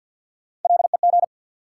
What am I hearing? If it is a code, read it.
Morse code, heard as a keyed tone, about 53 wpm, 700 Hz. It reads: LEG
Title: Three Letter Words - 50wpm